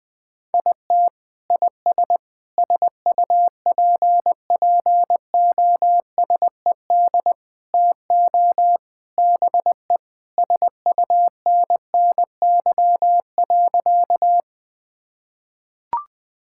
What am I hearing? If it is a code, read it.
Morse code, heard as a keyed tone, about 20 wpm, 700 Hz. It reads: IT IS SUPPOSED TO BE SUNNY.
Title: Sentences with 3-Word Phrases - 20wpm